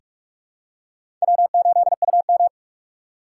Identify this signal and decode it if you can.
Morse code, heard as a keyed tone, about 46 wpm, 700 Hz. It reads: W8UM
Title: USA Call Signs - 4x Character Spacing - 45wpm